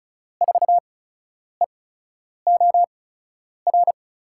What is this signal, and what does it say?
Morse code, heard as a keyed tone, about 36 wpm, spaced as for 10 wpm, 700 Hz. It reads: 4EOR
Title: USA Call Signs - 8x Character Spacing - 35wpm